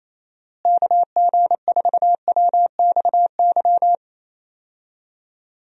Morse code, heard as a keyed tone, about 28 wpm, 700 Hz. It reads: KG4WXY
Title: USA Call Signs - 6 Characters - Repeated 3x - 28wpm